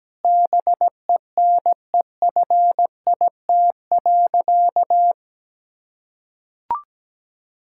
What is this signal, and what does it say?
Morse code, heard as a keyed tone, about 17 wpm, 700 Hz. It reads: BENEFIT.
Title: Sentences from Top 1000 Words - Review Part 6 - 17wpm